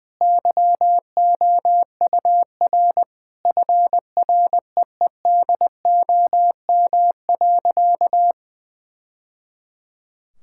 Morse code, heard as a keyed tone, about 20 wpm, 700 Hz. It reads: YOUR FREEDOM.
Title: Sentences from Top 1000 Words - Review Part 5 - 20wpm